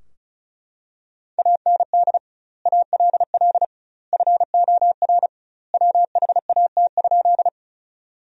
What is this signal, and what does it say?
Morse code, heard as a keyed tone, about 34 wpm, 700 Hz. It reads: AND ALL FOR WHAT?
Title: 2x Word-Spacing - Sentences from Top 100 Words - 35wpm